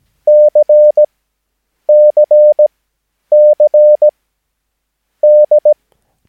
Morse code, heard as a keyed tone, about 17 wpm, spaced as for 8 wpm, 600 Hz. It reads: CCCD
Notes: Övningsmaterial med alfabetet A till Ö samt siffror 0 till 9 enligt International/Continental Morse Code.